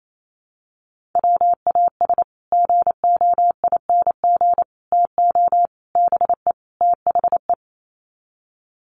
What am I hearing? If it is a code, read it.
Morse code, heard as a keyed tone, about 28 wpm, 700 Hz. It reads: WAS GOING TO BE THE
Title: Sets of 5 Words Encore2 28wpm